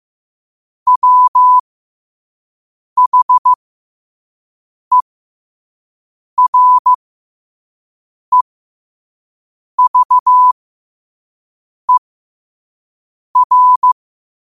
Morse code, heard as a keyed tone, about 15 wpm, spaced as for 5 wpm, 1000 Hz. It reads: WHEREVER
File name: qotdfiles / 5wpm